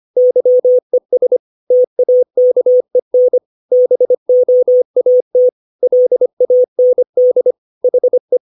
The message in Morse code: YES TAKEN BOAT LAND HE

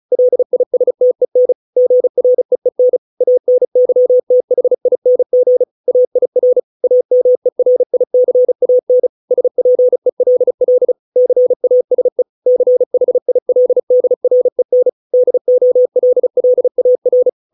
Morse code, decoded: LISTEN GREEN ANYTHING AIR AMERICAN SPELL CASE CHILDREN DOLLAR

